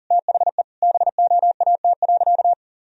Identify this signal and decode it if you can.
Morse code, heard as a keyed tone, about 37 wpm, 700 Hz. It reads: THE BOAT.
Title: Sentences from Top 1000 Words - Review Part 1 - 40wpm